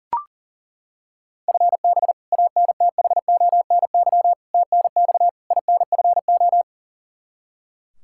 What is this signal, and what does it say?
Morse code, heard as a keyed tone, about 40 wpm, 690 Hz. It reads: FB ANTHONY TNX INFO
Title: Practice QSO - Exchange 3 - More Info - 40wpm